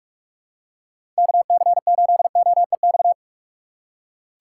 Morse code, heard as a keyed tone, about 46 wpm, 700 Hz. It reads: KC8OEX
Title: USA Call Signs - 6 Characters - 4x Character Spacing - 45wpm